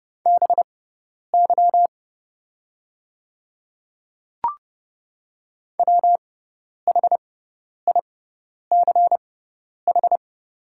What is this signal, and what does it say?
Morse code, heard as a keyed tone, about 30 wpm, spaced as for 10 wpm, 700 Hz. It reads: BY WHICH
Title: 6x Character Spacing - Top 200 Words - 30wpm